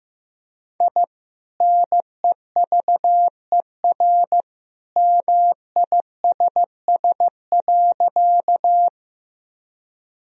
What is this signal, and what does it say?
Morse code, heard as a keyed tone, about 16 wpm, 700 Hz. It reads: I NEVER MISS.
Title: Sentences from Top 1000 Words - Review Part 3 - 15wpm